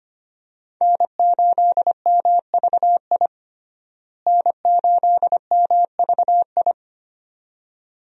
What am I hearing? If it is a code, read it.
Morse code, heard as a keyed tone, about 25 wpm, 700 Hz. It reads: N8MVI N8MVI
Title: USA Call Signs - 5 Characters - Repeated 2x - 25wpm